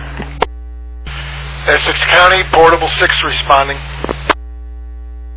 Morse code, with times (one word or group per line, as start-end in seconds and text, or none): none